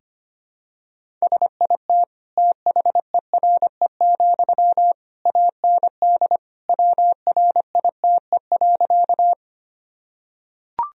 Morse code, heard as a keyed tone, about 25 wpm, 700 Hz. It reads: SIT THERE, AND WRITE.
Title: Sentences from Top 1000 Words - Review Part 1 - 25wpm